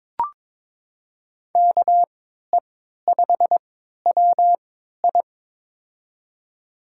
Morse code, heard as a keyed tone, about 21 wpm, spaced as for 12 wpm, 700 Hz. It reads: KE5WI
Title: USA Call Signs - 5 Characters - 3x Character Spacing - 22wpm